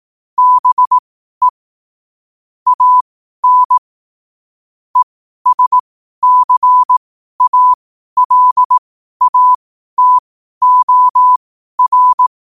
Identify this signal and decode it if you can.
Morse code, heard as a keyed tone, about 18 wpm, spaced as for 12 wpm, 1000 Hz. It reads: BE AN ESCALATOR
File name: qotdfiles / 12wpm